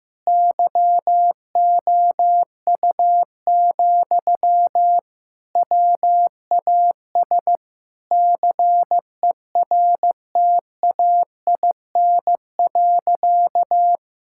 Morse code, heard as a keyed tone, about 15 wpm, 700 Hz. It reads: YOU, WAS CERTAIN.
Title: Sentences from Top 1000 Words - Review Part 6 - 15wpm